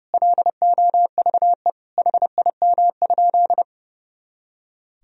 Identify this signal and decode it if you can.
Morse code, heard as a keyed tone, about 30 wpm, 700 Hz. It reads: LOVE HIM?
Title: Sentences from Top 400 Words and Abbreviations 30wpm